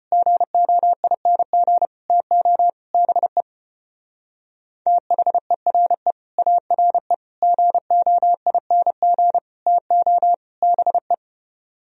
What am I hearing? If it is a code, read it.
Morse code, heard as a keyed tone, about 29 wpm, 700 Hz. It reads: GOING TO BE THERE ARE GOING TO BE
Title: Speed-Racing - Sets of 5 Words 25wpm